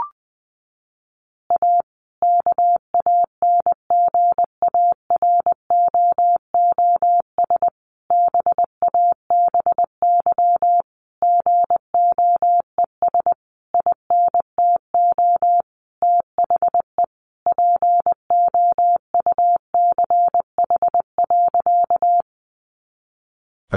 Morse code, heard as a keyed tone, about 20 wpm, 700 Hz. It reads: A KANGAROOS BABY GOES INTO THE POUCH.
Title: Easy Phrases 20wpm